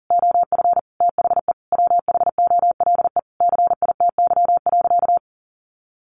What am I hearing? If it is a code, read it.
Morse code, heard as a keyed tone, about 37 wpm, 700 Hz. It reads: OF THE WHOLE CITY.
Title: Sentences from Top 1400 Words 40wpm